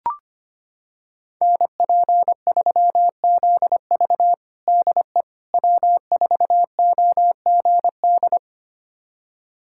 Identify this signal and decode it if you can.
Morse code, heard as a keyed tone, about 25 wpm, 700 Hz. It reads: NP3ZV DE W4OGD
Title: USA Callsign DE Callsign - 5 Characters - 25wpm